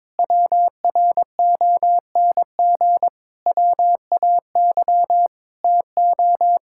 Morse code, heard as a keyed tone, about 22 wpm, 700 Hz. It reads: WRONG WAY TO
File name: practice / 22wpm / Sentences with 8-Word Phrases - 22wpm